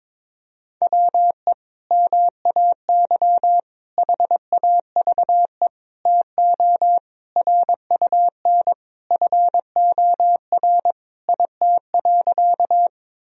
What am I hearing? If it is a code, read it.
Morse code, heard as a keyed tone, about 22 wpm, 700 Hz. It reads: WE MAY HAVE TO RUN FOR IT.
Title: Sentences from Top 1000 Words - Review Part 1 - 22wpm